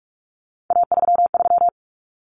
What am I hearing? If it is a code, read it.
Morse code, heard as a keyed tone, about 44 wpm, 700 Hz. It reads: A33
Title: Common Cut Numbers - 3-Digit 45wpm